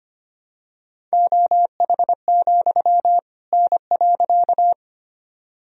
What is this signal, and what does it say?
Morse code, heard as a keyed tone, about 25 wpm, 700 Hz. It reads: OH, N.
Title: Sentences from Top 500 Words and Abbreviations 25wpm